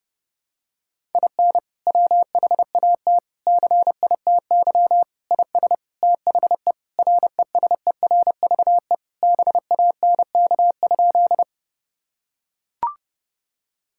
Morse code, heard as a keyed tone, about 28 wpm, 700 Hz. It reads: IN WHAT CITY IS THE RESERVE BANK?
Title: Sentences from Top 1000 Words - Review Part 6 - 30wpm